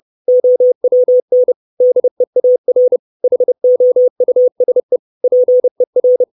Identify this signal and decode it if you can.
Morse code, heard as a keyed tone, about 30 wpm, 500 Hz. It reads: OWN DEAR HOUSE PER
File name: 632 Words 30WPM